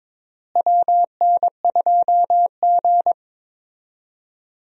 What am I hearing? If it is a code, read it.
Morse code, heard as a keyed tone, about 22 wpm, 700 Hz. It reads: WN2G
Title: USA Call Signs - 4 Characters - Repeated 2x - 22wpm